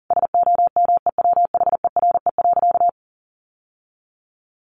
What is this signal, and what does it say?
Morse code, heard as a keyed tone, about 37 wpm, 700 Hz. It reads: SOMEWHERE.
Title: Sentences from Top 1400 Words 40wpm